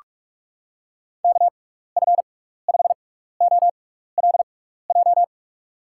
Morse code, heard as a keyed tone, about 44 wpm, spaced as for 15 wpm, 700 Hz. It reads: KF5OLJ
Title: USA Call Signs - 6 Characters - 6x Character Spacing - 45wpm